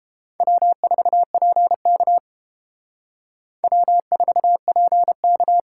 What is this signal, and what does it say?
Morse code, heard as a keyed tone, about 30 wpm, 700 Hz. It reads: W4PK W4PK